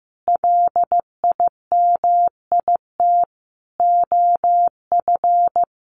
Morse code, heard as a keyed tone, about 15 wpm, 700 Hz. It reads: LIMIT OF
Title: Sentences from Top 1500 Words 15wpm